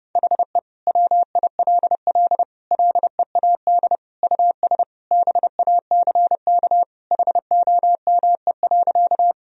Morse code, heard as a keyed tone, about 30 wpm, 700 Hz. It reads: HE WILL LEAD US BACK HOME.
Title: Sentences from Top 1000 Words - Review Part 2 - 30wpm